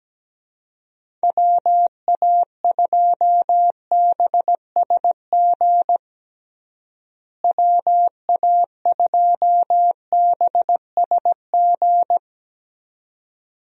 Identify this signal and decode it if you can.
Morse code, heard as a keyed tone, about 17 wpm, 700 Hz. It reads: WA2BSG WA2BSG
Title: USA Call Signs - 6 Characters - Repeated 2x - 17wpm